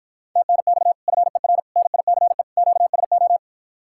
Morse code, heard as a keyed tone, about 54 wpm, 700 Hz. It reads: TNX FER NICE QSO